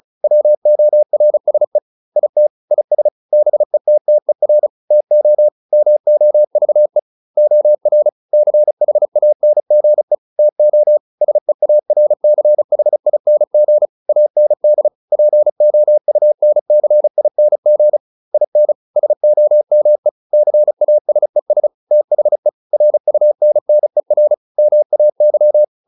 Morse code, heard as a keyed tone, about 34 wpm, 600 Hz. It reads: WORSE IT IS BETTER TO MOVE OR CHANGE TO SEARCHING AND POUNCING IN SOME CASES THE RUNNER MAY
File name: SS309 35